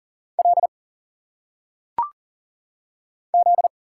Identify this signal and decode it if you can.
Morse code, heard as a keyed tone, about 40 wpm, 700 Hz. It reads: L Z
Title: Single Letters - Phonetic Pronunciation 40wpm